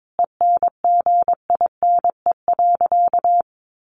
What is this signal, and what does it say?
Morse code, heard as a keyed tone, about 22 wpm, 700 Hz. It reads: ENGINE.
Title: Sentences from Top 1500 Words 22wpm